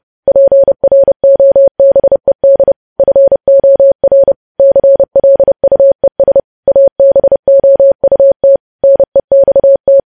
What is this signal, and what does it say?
Morse code, heard as a keyed tone, about 30 wpm, 550 Hz. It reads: PROBED FOR CLUES ABOUT NEXT
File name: podcast / News headlines-1x-30x30